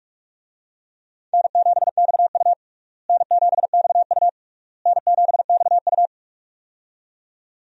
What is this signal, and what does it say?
Morse code, heard as a keyed tone, about 43 wpm, 700 Hz. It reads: N7XU N7XU N7XU
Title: USA Call Signs - 4 Characters - Repeated 3x - 45wpm